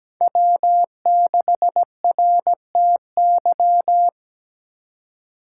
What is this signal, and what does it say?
Morse code, heard as a keyed tone, about 17 wpm, 700 Hz. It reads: W6RTY
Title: USA Call Signs - 5 Characters - 4x Character Spacing - 17wpm